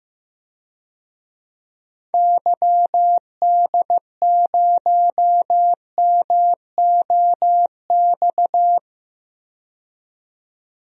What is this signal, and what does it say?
Morse code, heard as a keyed tone, about 15 wpm, 700 Hz. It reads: YD0MOX
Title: DXCC-100 Call Signs - 6 Characters - 15wpm